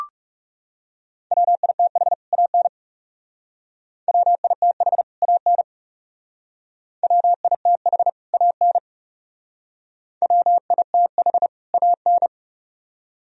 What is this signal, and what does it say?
Morse code, heard as a keyed tone, about 36 wpm, 700 Hz. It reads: WITH AN WITH AN WITH AN WITH AN
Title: Speed-Racing - Sets of 2 Words 30wpm